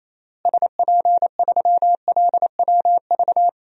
Morse code, heard as a keyed tone, about 28 wpm, 700 Hz. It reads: SP3LWV
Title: DXCC-100 Call Signs - 6 Characters - Repeated 2x - 28wpm